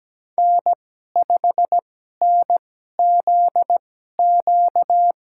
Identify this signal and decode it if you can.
Morse code, heard as a keyed tone, about 17 wpm, spaced as for 12 wpm, 700 Hz. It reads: N5NZQ
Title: USA Call Signs - 5 Characters - 2x Character Spacing - 17wpm